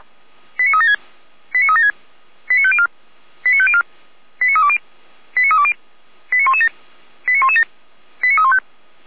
Five-tone selective call (ZVEI), heard as quick runs of five tones, tones of about 70 ms. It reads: 89376 89376 89504 89504 89320 89320 89117 89117 89315
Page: ZVEI